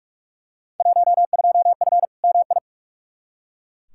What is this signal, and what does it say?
Morse code, heard as a keyed tone, about 43 wpm, 700 Hz. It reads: 12F MI
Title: Field Day Exchanges Encore - 45wpm